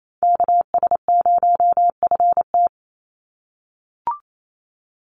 Morse code, heard as a keyed tone, about 28 wpm, 700 Hz. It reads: KS0FT
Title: USA Call Signs 28wpm